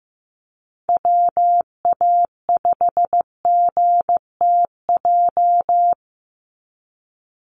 Morse code, heard as a keyed tone, about 15 wpm, 700 Hz. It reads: WA5GTJ
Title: USA Call Signs 15wpm